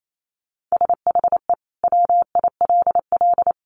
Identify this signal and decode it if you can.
Morse code, heard as a keyed tone, about 28 wpm, 700 Hz. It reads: SHE WILL
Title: Sentences from Top 1500 Words 28wpm